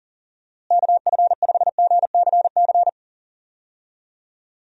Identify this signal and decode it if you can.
Morse code, heard as a keyed tone, about 40 wpm, 700 Hz. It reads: KF5GCC
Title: USA Call Signs - 6 Characters - Repeated 2x - 40wpm